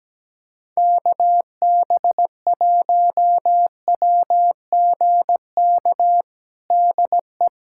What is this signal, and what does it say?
Morse code, heard as a keyed tone, about 17 wpm, 700 Hz. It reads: KB1WGK DE
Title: USA Callsign DE Callsign - 6 Characters - 17wpm